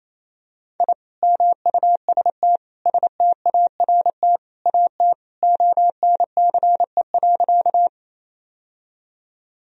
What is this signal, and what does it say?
Morse code, heard as a keyed tone, about 27 wpm, 700 Hz. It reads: I MUST START AT ONCE.
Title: Sentences from Top 1000 Words - Review Part 1 - 28wpm